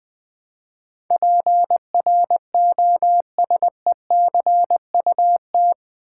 Morse code, heard as a keyed tone, about 20 wpm, 700 Hz. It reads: PROSECUT
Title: Build A Word - 4 Letters At A Time - 11-Letter Words - 20wpm